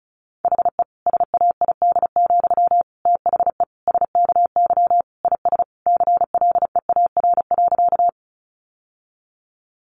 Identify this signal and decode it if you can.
Morse code, heard as a keyed tone, about 34 wpm, 700 Hz. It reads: HE SAID, THE SKY IS CLEAR.
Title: Sentences from Top 1200 Words 35wpm